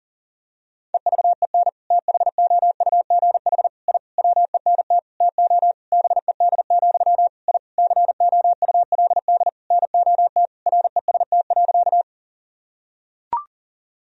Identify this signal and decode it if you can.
Morse code, heard as a keyed tone, about 40 wpm, 700 Hz. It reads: EVEN THOUGH I WENT TO BED, I COULD NOT REST.
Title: Sentences from Top 1000 Words - Review Part 2 - 40wpm